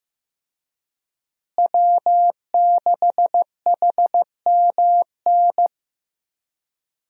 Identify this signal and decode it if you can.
Morse code, heard as a keyed tone, about 16 wpm, 700 Hz. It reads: W6HMN
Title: USA Call Signs - 5 Characters - Repeated 3x - 15wpm